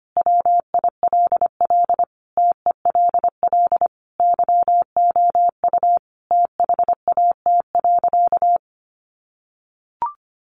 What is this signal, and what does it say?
Morse code, heard as a keyed tone, about 25 wpm, 700 Hz. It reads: WILL TELL YOU THAT.
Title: Sentences from Top 300 Words 25wpm